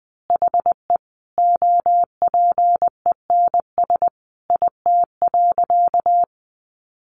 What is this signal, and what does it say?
Morse code, heard as a keyed tone, about 19 wpm, 700 Hz. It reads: HE OPENS IT.
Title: Additional Easy Phrases 20wpm